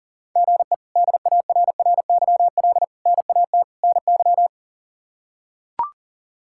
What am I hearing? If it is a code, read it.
Morse code, heard as a keyed tone, about 37 wpm, 700 Hz. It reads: GE DARRYL NAT NY
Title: K1USN SST QSO Practice - 40wpm